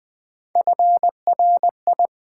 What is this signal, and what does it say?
Morse code, heard as a keyed tone, about 20 wpm, 700 Hz. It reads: FRI
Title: Build A Word - 3 Letters At A Time - 8-Letter Words - 20wpm